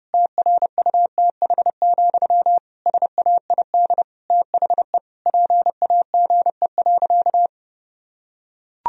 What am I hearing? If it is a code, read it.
Morse code, heard as a keyed tone, about 32 wpm, 700 Hz. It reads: TRUTH, SAID THE PAGE.
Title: Sentences from Top 1000 Words - Review Part 1 - 30wpm